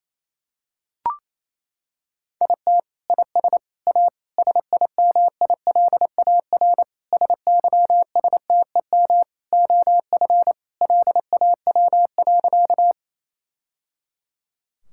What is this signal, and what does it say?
Morse code, heard as a keyed tone, about 28 wpm, 700 Hz. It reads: IT IS A SIMILAR SYSTEM OF LAW.
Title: Sentences from Top 1000 Words - Review Part 5 - 28wpm